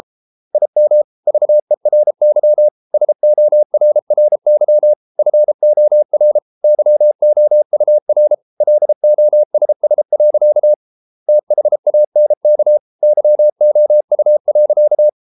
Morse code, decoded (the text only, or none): IM VERY SORRY FOR YOUR LOSS. THANK YOU.